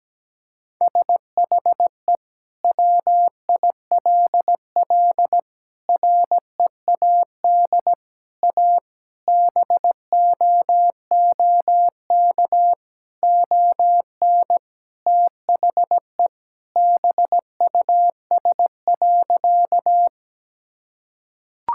Morse code, heard as a keyed tone, about 17 wpm, 700 Hz. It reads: SHE WILL READ A BOOK ON THE BUS.
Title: Sentences from Top 1000 Words - Review Part 6 - 17wpm